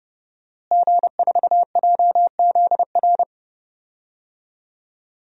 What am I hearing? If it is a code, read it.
Morse code, heard as a keyed tone, about 30 wpm, 700 Hz. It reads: G4JZR
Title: DXCC-100 Call Signs - 5 Characters - Repeated 3x - 30wpm